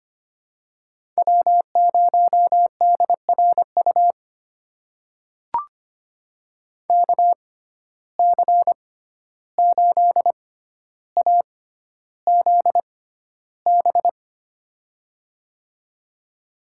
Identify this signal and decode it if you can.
Morse code, heard as a keyed tone, about 25 wpm, spaced as for 12 wpm, 700 Hz. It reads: W0DRU KC8AZB
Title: USA Call Signs - 6x Character Spacing - 25wpm